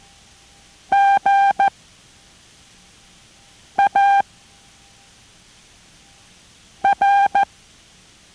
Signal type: Morse code